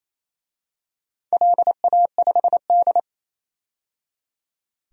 Morse code, heard as a keyed tone, about 28 wpm, 700 Hz. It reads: LA5D
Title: DXCC-100 Call Signs - 4 Characters - 28wpm